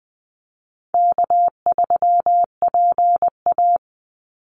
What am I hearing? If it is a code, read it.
Morse code, heard as a keyed tone, about 20 wpm, 700 Hz. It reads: K3PA
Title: Easy Call Signs Repeated 3x - 20wpm